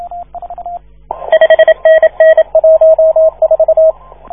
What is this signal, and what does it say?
Morse code, heard as a keyed tone, about 27 wpm, 640 Hz, with radio noise. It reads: 5NN 14
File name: 1esempio-telegrafia